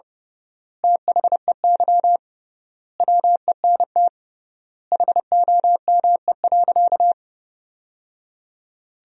Morse code, heard as a keyed tone, about 30 wpm, 700 Hz. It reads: THEY WENT HOME.
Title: 3x Word-Spacing - Sentences from Top 200 Words - 30wpm